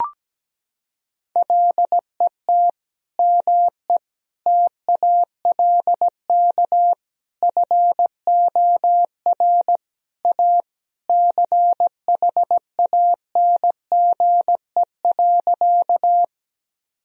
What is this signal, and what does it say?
Morse code, heard as a keyed tone, about 17 wpm, 700 Hz. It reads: LET ME TALK FOR A CHANGE.
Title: Sentences from Top 1000 Words - Review Part 1 - 17wpm